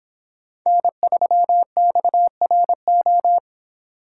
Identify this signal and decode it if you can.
Morse code, heard as a keyed tone, about 26 wpm, 700 Hz. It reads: N3XRO